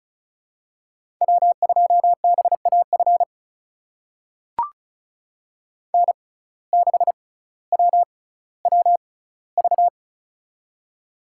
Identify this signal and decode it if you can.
Morse code, heard as a keyed tone, about 36 wpm, spaced as for 18 wpm, 700 Hz. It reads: W2BAF N6WWV
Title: USA Call Signs - 5 Characters - 6x Character Spacing - 35wpm